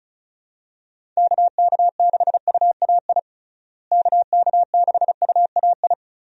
Morse code, heard as a keyed tone, about 36 wpm, 700 Hz. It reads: KK6UAI KK6UAI
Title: USA Call Signs - 6 Characters - Repeated 3x - 35wpm